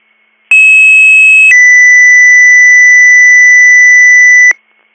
Two-tone page, heard as a one-second tone, then a three-second tone, about 2600 then 2000 Hz. It is heard from 0.5 to 4.5 s.